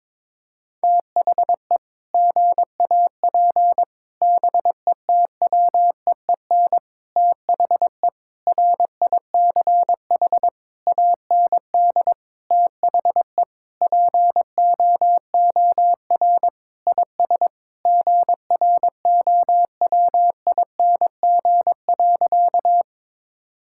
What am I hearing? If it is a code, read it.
Morse code, heard as a keyed tone, about 21 wpm, 700 Hz. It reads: THE GAP BETWEEN THE RICH AND THE POOR IS GROWING.
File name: practice / 22wpm / Sentences with 8-Word Phrases - 22wpm